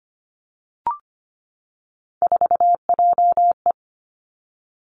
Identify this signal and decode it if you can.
Morse code, heard as a keyed tone, about 25 wpm, 700 Hz. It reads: 4JE
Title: Three Challenging Letter-Number Combinations 25wpm